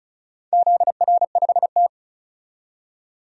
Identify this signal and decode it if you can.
Morse code, heard as a keyed tone, about 34 wpm, 700 Hz. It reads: ZR5T
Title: DXCC-100 Call Signs - 4 Characters - 35wpm